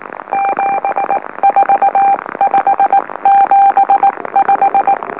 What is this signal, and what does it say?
Morse code, heard as a keyed tone, about 19 wpm, spaced as for 16 wpm, 790 Hz. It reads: 74575
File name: M04- -2345utc-LOLO